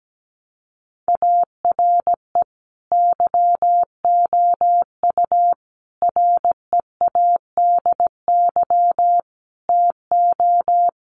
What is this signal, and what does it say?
Morse code, heard as a keyed tone, about 17 wpm, 700 Hz. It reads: ARE YOU READY TO
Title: Sentences from Top 400 Words 17wpm